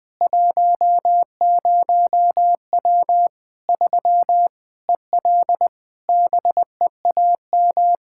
Morse code, heard as a keyed tone, about 20 wpm, 700 Hz. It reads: 10W 3 EL BEAM
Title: Practice QSO - Exchange 2 - Info - 20wpm